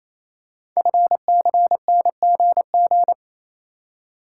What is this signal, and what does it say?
Morse code, heard as a keyed tone, about 28 wpm, 700 Hz. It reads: FCNGG
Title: Code Groups - 5 Letters - 28wpm